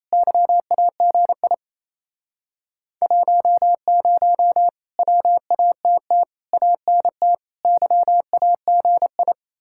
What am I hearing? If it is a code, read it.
Morse code, heard as a keyed tone, about 37 wpm, 700 Hz. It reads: YAGI 10 WATT ANT YAGI